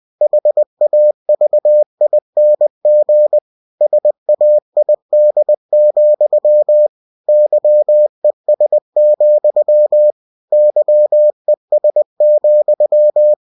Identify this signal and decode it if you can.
Morse code, heard as a keyed tone, about 20 wpm, 600 Hz. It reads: HAVING SAID, YES, YES,
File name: cw-books / winnie-the-pooh / winnie-20wpm-page-0031